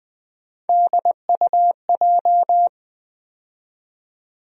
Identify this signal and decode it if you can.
Morse code, heard as a keyed tone, about 20 wpm, 700 Hz. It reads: DUJ
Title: USA Call Sign Postfixes - 3 Letters 20wpm